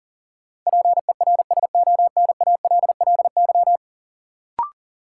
Morse code, heard as a keyed tone, about 40 wpm, 700 Hz. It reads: PERSONALLY
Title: Sentences from Top 1400 Words - New Word Drill Alt - 40wpm